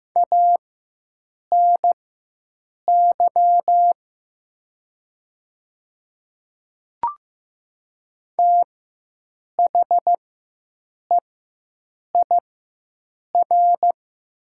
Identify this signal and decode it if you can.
Morse code, heard as a keyed tone, about 15 wpm, spaced as for 7 wpm, 700 Hz. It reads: ANY THEIR
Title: 4x Character Spacing - Top 300 Words - 15wpm